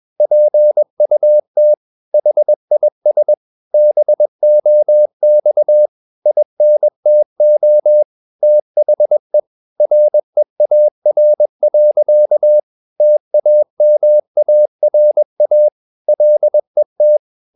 Morse code, heard as a keyed tone, about 21 wpm, 600 Hz. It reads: PUT HIS BOX INTO THE REAR. TAMARA LET
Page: Romanen Never af Ken Follet på engelsk.